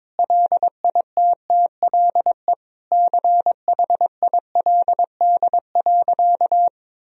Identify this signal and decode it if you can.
Morse code, heard as a keyed tone, about 22 wpm, 700 Hz. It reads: LITTLE CHILD.
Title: Sentences from Top 1000 Words - Review Part 1 - 22wpm